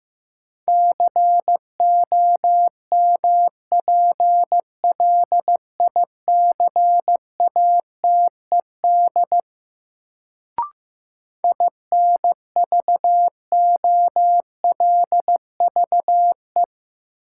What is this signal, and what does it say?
Morse code, heard as a keyed tone, about 15 wpm, 700 Hz. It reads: COMPLICATED INVOLVE
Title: Sentences from Top 1200 Words - New Word Drill Alt - 15wpm